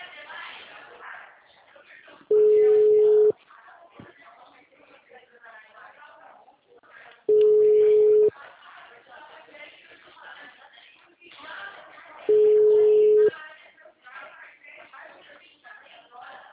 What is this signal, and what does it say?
Ringback tone (Europe, 425 Hz), with speech.
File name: out- -2013- - - .75446